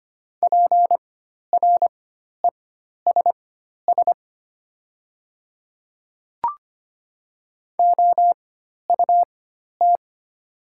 Morse code, heard as a keyed tone, about 25 wpm, spaced as for 12 wpm, 700 Hz. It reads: PRESS OUT
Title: 4x Character Spacing - Top 300 Words - 25wpm